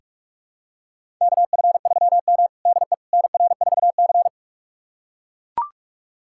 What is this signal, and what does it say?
Morse code, heard as a keyed tone, about 42 wpm, 700 Hz. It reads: KF3M DE NR4C
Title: USA Callsign DE Callsign - 4 Characters - 45wpm